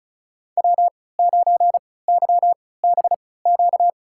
Morse code, heard as a keyed tone, about 34 wpm, spaced as for 20 wpm, 700 Hz. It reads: W9YBQ
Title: USA Call Signs - 5 Characters - 3x Character Spacing - 35wpm